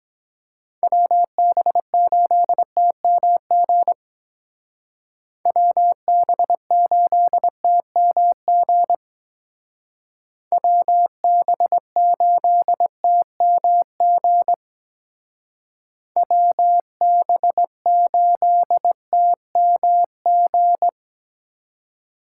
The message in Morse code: WB8TMG WB8TMG WB8TMG WB8TMG

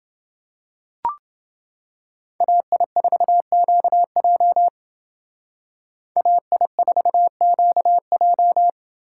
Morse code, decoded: AI4QJ AI4QJ

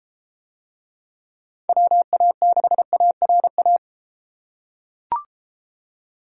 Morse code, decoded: WA6ARA

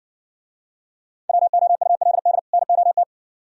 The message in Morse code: WOULD NOT

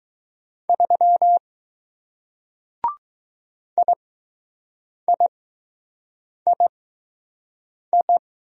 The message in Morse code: 3 IIII

